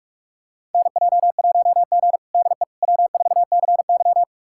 Morse code, heard as a keyed tone, about 42 wpm, 700 Hz. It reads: NJ1G DE W4CY
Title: USA Callsign DE Callsign - 4 Characters - 45wpm